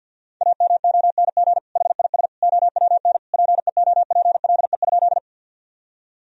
Morse code, heard as a keyed tone, about 50 wpm, 700 Hz. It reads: AMONG HIS OWN PEOPLE?
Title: Sentences from Top 1000 Words - Review Part 3 - 50wpm